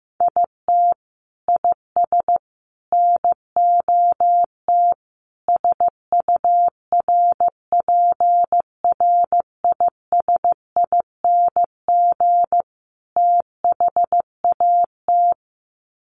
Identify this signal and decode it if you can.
Morse code, heard as a keyed tone, about 15 wpm, 700 Hz. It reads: IT IS NOT SURPRISING THAT
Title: Sets of 5 Words 15wpm